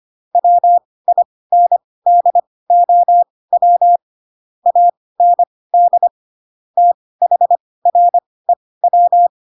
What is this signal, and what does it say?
Morse code, heard as a keyed tone, about 25 wpm, spaced as for 18 wpm, 700 Hz. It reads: WINDOW AND THREW
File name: Pinocchio / Pinocchio ch7 12 25 ik1hge